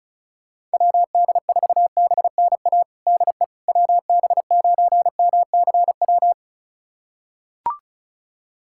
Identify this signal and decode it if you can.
Morse code, heard as a keyed tone, about 36 wpm, 700 Hz. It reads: WD4BNA DE WB9MCW
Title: USA Callsign DE Callsign - 6 Characters - 35wpm